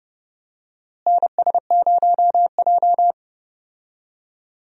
Morse code, heard as a keyed tone, about 30 wpm, 700 Hz. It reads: NS0J
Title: USA Call Signs - 4 Characters - Repeated 2x - 30wpm